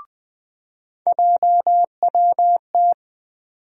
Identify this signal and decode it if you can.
Morse code, heard as a keyed tone, about 20 wpm, 700 Hz. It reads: JWT
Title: USA Call Sign Postfixes - 3 Letters 20wpm